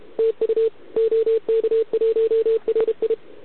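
Morse code, heard as a keyed tone, about 31 wpm, 440 Hz. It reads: TU OK1RI
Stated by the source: Contest CW from OK1RI on 80 m during the OKOM DX contest, recorded by a QRP station using a KX3 and a wire antenna.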